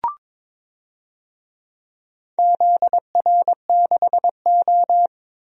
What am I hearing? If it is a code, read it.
Morse code, heard as a keyed tone, about 22 wpm, 700 Hz. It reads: ZR6O
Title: DXCC-100 Call Signs - 4 Characters - 22wpm